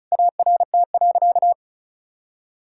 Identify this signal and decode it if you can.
Morse code, heard as a keyed tone, about 36 wpm, 700 Hz. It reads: ART.
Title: Sentences from Top 1000 Words - Review Part 6 - 35wpm